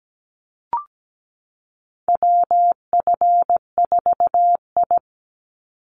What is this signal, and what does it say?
Morse code, heard as a keyed tone, about 17 wpm, 700 Hz. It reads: WF4I
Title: NA SOTA Chaser Callsigns 17wpm